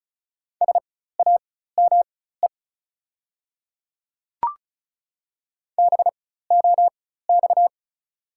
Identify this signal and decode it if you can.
Morse code, heard as a keyed tone, about 34 wpm, spaced as for 16 wpm, 690 Hz. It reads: SAME BOX
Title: 4x Character Spacing - Top 500 Words - 35wpm